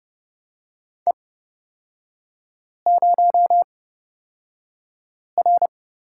Morse code, heard as a keyed tone, about 30 wpm, spaced as for 5 wpm, 700 Hz. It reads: E0R
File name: Lesson 37e - Letter x - Three Characters 30wpm